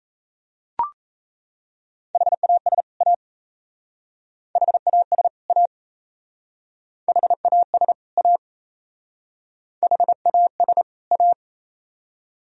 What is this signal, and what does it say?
Morse code, heard as a keyed tone, about 33 wpm, 700 Hz. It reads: HAS A HAS A HAS A HAS A
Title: Speed-Racing - Sets of 2 Words 28wpm